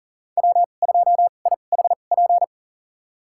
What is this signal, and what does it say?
Morse code, heard as a keyed tone, about 40 wpm, spaced as for 29 wpm, 700 Hz. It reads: W2IHP
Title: USA Call Signs - 5 Characters - 2x Character Spacing - 40wpm